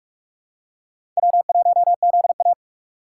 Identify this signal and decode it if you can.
Morse code, heard as a keyed tone, about 46 wpm, 700 Hz. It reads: W1ZA
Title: USA Call Signs - 4 Characters - 4x Character Spacing - 45wpm